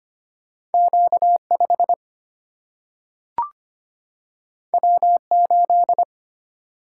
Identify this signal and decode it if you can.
Morse code, heard as a keyed tone, about 25 wpm, 700 Hz. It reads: Q 5 W 8
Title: Single-Letter Single-Number 25wpm